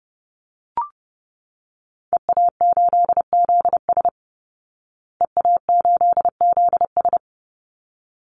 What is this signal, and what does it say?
Morse code, heard as a keyed tone, about 30 wpm, 700 Hz. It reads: EA8ZS EA8ZS
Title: Call Signs Repeated 3x - 30wpm